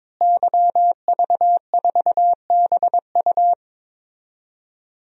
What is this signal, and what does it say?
Morse code, heard as a keyed tone, about 22 wpm, 700 Hz. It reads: YV4BU
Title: DXCC-100 Call Signs - 5 Characters - Repeated 3x - 22wpm